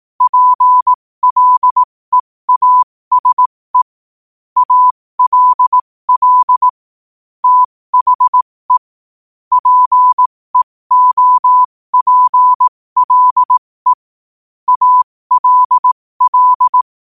Morse code, heard as a keyed tone, about 18 wpm, spaced as for 15 wpm, 1000 Hz. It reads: PLEASE ALL THE PEOPLE ALL
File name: qotdfiles / 15wpm